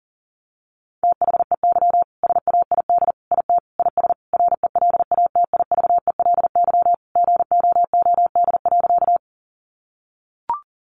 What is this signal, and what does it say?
Morse code, heard as a keyed tone, about 40 wpm, 700 Hz. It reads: THEY SAID IT IS RELATIVELY GOOD.
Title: Sentences from Top 1100 Words 40wpm